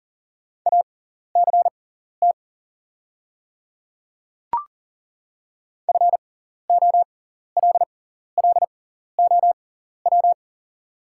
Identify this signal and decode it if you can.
Morse code, heard as a keyed tone, about 38 wpm, spaced as for 14 wpm, 690 Hz. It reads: ACT FOLLOW
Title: 6x Character Spacing - Top 200 Words - 40wpm